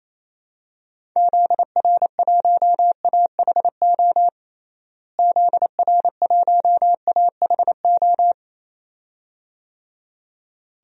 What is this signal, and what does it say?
Morse code, heard as a keyed tone, about 27 wpm, 700 Hz. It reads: ZR1AHO ZR1AHO
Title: DXCC-100 Call Signs - 6 Characters - Repeated 2x - 28wpm